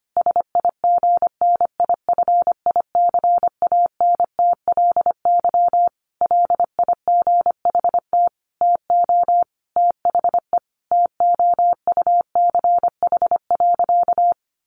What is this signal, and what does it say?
Morse code, heard as a keyed tone, about 25 wpm, 700 Hz. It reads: SIGNIFICANTLY LIGHT TO THE TOUCH.
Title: Sentences from Top 1500 Words 25wpm